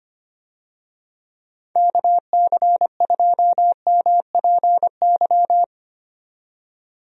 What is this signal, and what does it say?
Morse code, heard as a keyed tone, about 25 wpm, 700 Hz. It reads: KC2MPY
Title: USA Call Signs - 6 Characters - 25wpm